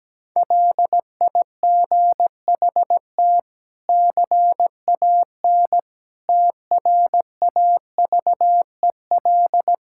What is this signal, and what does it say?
Morse code, heard as a keyed tone, about 17 wpm, 700 Hz. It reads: LIGHT CAN TRAVEL
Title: Sentences from Top 1000 Words - Review Part 6 - 17wpm